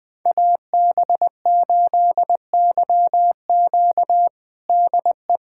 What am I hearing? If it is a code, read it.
Morse code, heard as a keyed tone, about 21 wpm, 700 Hz. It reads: AB8YQ DE
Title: USA Callsign DE Callsign 20wpm